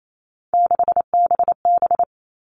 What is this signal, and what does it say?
Morse code, heard as a keyed tone, about 28 wpm, 700 Hz. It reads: 6BB
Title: All Cut Numbers - 3-Digit 28wpm